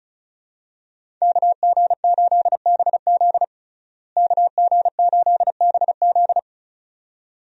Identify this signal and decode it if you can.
Morse code, heard as a keyed tone, about 36 wpm, 700 Hz. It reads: KG8BZ KG8BZ
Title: USA Call Signs - 5 Characters - Repeated 2x - 35wpm